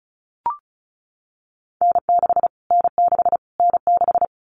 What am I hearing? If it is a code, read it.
Morse code, heard as a keyed tone, about 34 wpm, 700 Hz. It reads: N6 N6 N6
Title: Callsign Prefixes for Top 25 DXCC - Repeated 3x - 35wpm